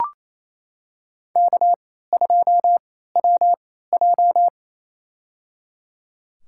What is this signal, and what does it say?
Morse code, heard as a keyed tone, about 28 wpm, spaced as for 16 wpm, 700 Hz. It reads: K2WJ
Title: USA Call Signs - 4 Characters - 3x Character Spacing - 28wpm